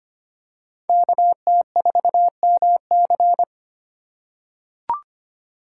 Morse code, heard as a keyed tone, about 25 wpm, 700 Hz. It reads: KT4MC
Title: USA Call Signs - 5 Characters - 3x Character Spacing - 25wpm